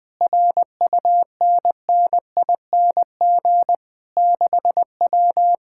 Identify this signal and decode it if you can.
Morse code, heard as a keyed tone, about 20 wpm, 700 Hz. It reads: RUNNING 6W
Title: Speed-Racing Common QSO Phrases - 20wpm